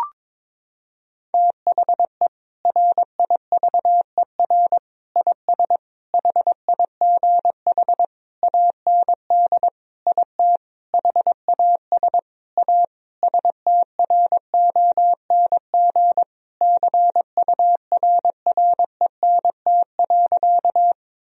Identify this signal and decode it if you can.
Morse code, heard as a keyed tone, about 22 wpm, 700 Hz. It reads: THE RIVER IS HIGH AND IT HAS A STRONG CURRENT.
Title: Sentences from Top 1000 Words - Review Part 4 - 22wpm